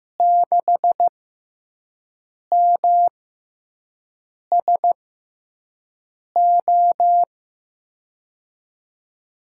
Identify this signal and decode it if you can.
Morse code, heard as a keyed tone, about 15 wpm, spaced as for 5 wpm, 700 Hz. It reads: 6MSO
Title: USA Call Signs - 6 Characters - 6x Character Spacing - 15wpm